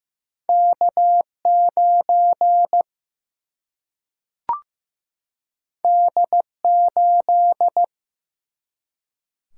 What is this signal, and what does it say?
Morse code, heard as a keyed tone, about 16 wpm, 700 Hz. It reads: K9 D8
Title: Single-Letter Single-Number 15wpm